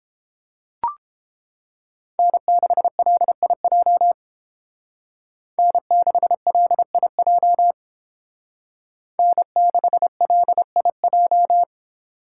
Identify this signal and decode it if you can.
Morse code, heard as a keyed tone, about 29 wpm, 700 Hz. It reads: N6LIJ N6LIJ N6LIJ